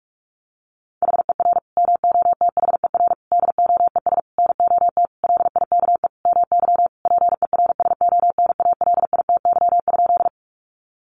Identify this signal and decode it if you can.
Morse code, heard as a keyed tone, about 44 wpm, 700 Hz. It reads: HER MOTHER DOES NOT LIKE MY PERSONALITY?
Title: Sentences from Top 1400 Words 45wpm